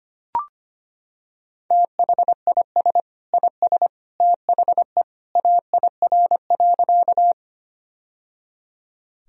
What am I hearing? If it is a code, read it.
Morse code, heard as a keyed tone, about 25 wpm, 700 Hz. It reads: THIS IS THE AIR.
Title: Sentences from Top 1000 Words - Review Part 1 - 25wpm